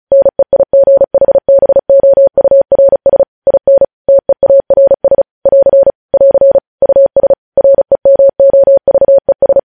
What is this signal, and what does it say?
Morse code, heard as a keyed tone, about 36 wpm, 550 Hz. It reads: NEIGHBOURS IN TEARS <AR> <AR> US REMOVES
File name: podcast / News headlines-1x-35x35